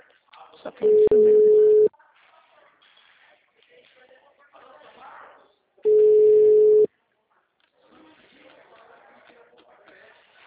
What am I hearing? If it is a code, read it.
Ringback tone (Europe, 425 Hz), with speech.
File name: out- -2014- - - .24889